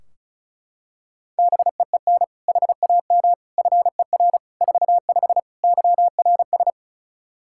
Morse code, heard as a keyed tone, about 36 wpm, 700 Hz. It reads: BEEN HAM FER 45 YRS
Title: Practice QSO - Exchange 3 - More Info - 35wpm